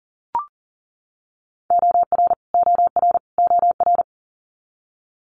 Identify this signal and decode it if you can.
Morse code, heard as a keyed tone, about 39 wpm, 690 Hz. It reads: OR OR OR
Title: Multi-letter Phonemes Repeated 3x 40wpm